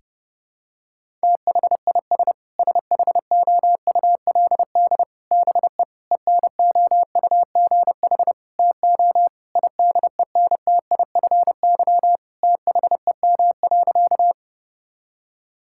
Morse code, heard as a keyed tone, about 32 wpm, 700 Hz. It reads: THIS SHOULD BE ENOUGH TO IDENTIFY THEM.
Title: Sentences from Top 1000 Words - Review Part 6 - 30wpm